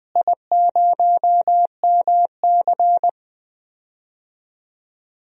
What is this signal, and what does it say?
Morse code, heard as a keyed tone, about 21 wpm, 700 Hz. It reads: I0MC
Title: DXCC-100 Call Signs - 4 Characters - Repeated 3x - 20wpm